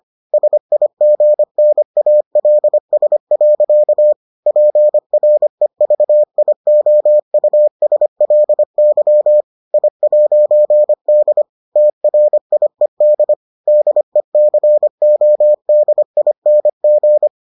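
Morse code, heard as a keyed tone, about 25 wpm, 600 Hz. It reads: SIGNALS. PREVIOUSLY I'D TRIED DECODING
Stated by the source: Morse edition of the Foundations of Amateur Radio episodes.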